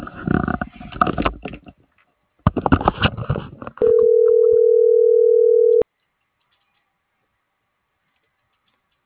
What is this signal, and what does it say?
Ringback tone (North America), 440 + 480 Hz.